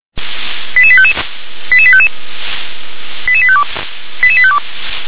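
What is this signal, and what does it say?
Five-tone selective call (ZVEI), heard as quick runs of five tones, tones of about 70 ms. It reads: 88855 88855 88852 88852
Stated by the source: Five-tone ZVEI selective call 88078, received by SDR.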